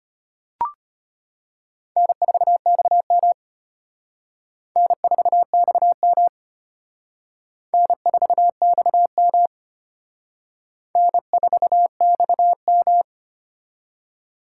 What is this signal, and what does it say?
Morse code, heard as a keyed tone, about 31 wpm, 700 Hz. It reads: N4XM N4XM N4XM N4XM
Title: Speed Racing - USA 1x2 Call Signs 25wpm